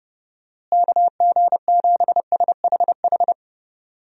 Morse code, heard as a keyed tone, about 29 wpm, 700 Hz. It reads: KG7SHH
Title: USA Call Signs - 6 Characters - Repeated 3x - 30wpm